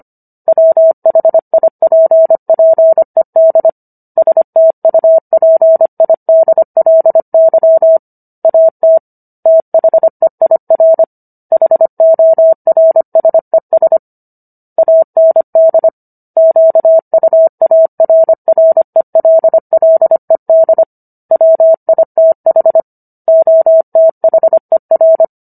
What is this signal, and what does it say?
Morse code, heard as a keyed tone, about 25 wpm, 650 Hz. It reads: WHIPPED STUPIDLY AT THEIR HORSES AND QUARRELLED WITH OTHER